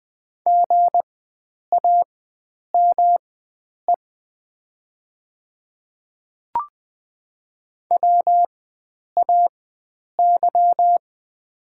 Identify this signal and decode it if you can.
Morse code, heard as a keyed tone, about 20 wpm, spaced as for 9 wpm, 700 Hz. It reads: GAME WAY
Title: 4x Character Spacing - Top 500 Words - 20wpm